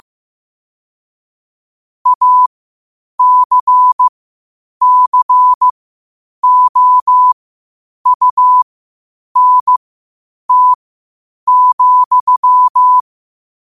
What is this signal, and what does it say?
Morse code, heard as a keyed tone, about 15 wpm, spaced as for 8 wpm, 1000 Hz. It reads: ACCOUNT,
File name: qotdfiles / 8wpm